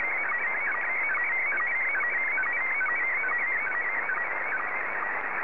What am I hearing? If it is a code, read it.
SSTV (slow-scan television), Scottie 1.